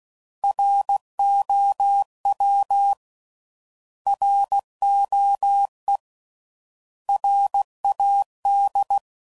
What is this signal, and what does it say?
Morse code, heard as a keyed tone, about 16 wpm, 800 Hz. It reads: ROW ROE RAD
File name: Prefixes / By First Letter / Rs